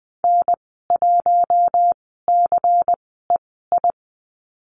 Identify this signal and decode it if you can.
Morse code, heard as a keyed tone, about 20 wpm, spaced as for 15 wpm, 700 Hz. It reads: N1CEI
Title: 140 Lesson 23g - Letter f - Call Signs 20wpm